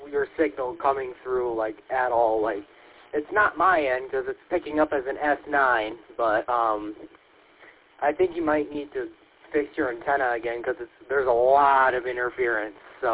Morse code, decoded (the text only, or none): none